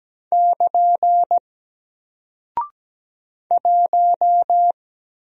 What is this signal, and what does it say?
Morse code, heard as a keyed tone, about 17 wpm, 700 Hz. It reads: <KN> 1
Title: Single Character (Letter:Number:Punctuation:Prosign) Alt 17wpm